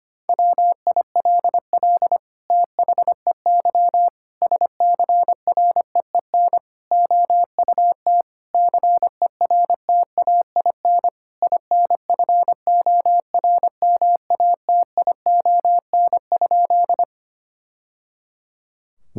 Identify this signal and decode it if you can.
Morse code, heard as a keyed tone, about 25 wpm, 700 Hz. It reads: WILL THEY SCREEN OUT CERTAIN INFORMATION?
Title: Sentences from Top 1000 Words - Review Part 6 - 25wpm